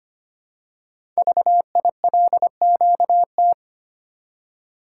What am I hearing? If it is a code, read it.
Morse code, heard as a keyed tone, about 25 wpm, 700 Hz. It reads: VILQT
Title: Code Groups - 5 Letters - 25wpm